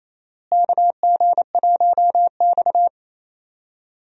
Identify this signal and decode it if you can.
Morse code, heard as a keyed tone, about 28 wpm, 700 Hz. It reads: KG1X
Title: USA Call Signs - 4 Characters - Repeated 2x - 28wpm